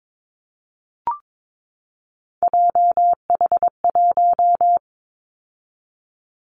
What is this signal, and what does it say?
Morse code, heard as a keyed tone, about 22 wpm, 700 Hz. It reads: JH1
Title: Callsign Prefixes for Top 25 DXCC - 22wpm